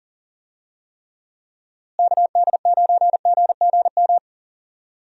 Morse code, heard as a keyed tone, about 39 wpm, 700 Hz. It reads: KD9GGM
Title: USA Call Signs - 6 Characters - Repeated 3x - 40wpm